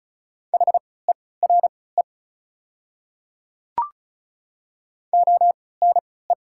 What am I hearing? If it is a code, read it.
Morse code, heard as a keyed tone, about 36 wpm, spaced as for 20 wpm, 690 Hz. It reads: HERE ONE
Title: 3x Character Spacing - Top 300 Words - 35wpm